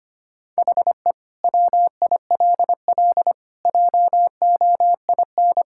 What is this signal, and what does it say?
Morse code, heard as a keyed tone, about 25 wpm, 700 Hz. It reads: HE WILL JOIN
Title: Sentences from Top 1000 Words - Review Part 6 - 25wpm